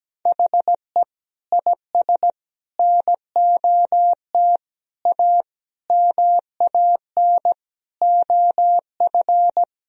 Morse code, heard as a keyed tone, about 17 wpm, 700 Hz. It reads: HE IS NOT A MAN OF
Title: Sentences from Top 1000 Words - Review Part 4 - 17wpm